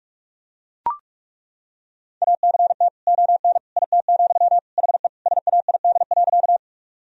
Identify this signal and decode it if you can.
Morse code, heard as a keyed tone, about 41 wpm, 700 Hz. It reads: ACT ON IT, HE SAID.
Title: Sentences from Top 1000 Words - Review Part 1 - 45wpm